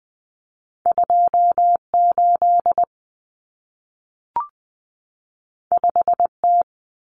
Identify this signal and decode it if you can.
Morse code, heard as a keyed tone, about 20 wpm, 700 Hz. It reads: 28 5T
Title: Common Cut Numbers - 2-Digit 20wpm